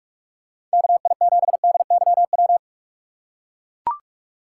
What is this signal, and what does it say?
Morse code, heard as a keyed tone, about 44 wpm, 700 Hz. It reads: KI7DYW
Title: USA Call Signs - 6 Characters - 6x Character Spacing - 45wpm